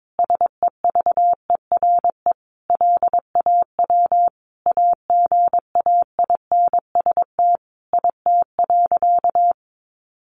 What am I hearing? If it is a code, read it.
Morse code, heard as a keyed tone, about 22 wpm, 700 Hz. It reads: SEVERE LAW AGAINST IT.
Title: Sentences from Top 1400 Words 22wpm